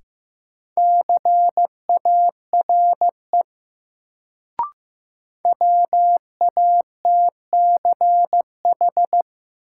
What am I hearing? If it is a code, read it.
Morse code, heard as a keyed tone, about 15 wpm, 700 Hz. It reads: CARE WATCH
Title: Sentences from Top 300 Words - New Word Drill - 15wpm